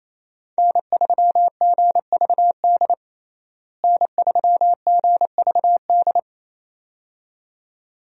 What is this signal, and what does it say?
Morse code, heard as a keyed tone, about 28 wpm, 700 Hz. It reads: N3GVD N3GVD
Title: USA Call Signs - 5 Characters - Repeated 2x - 28wpm